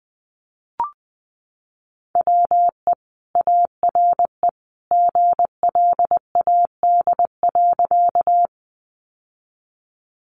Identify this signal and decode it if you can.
Morse code, heard as a keyed tone, about 20 wpm, 700 Hz. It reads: WE ARE GLAD.
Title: Sentences from Top 800 Words 20wpm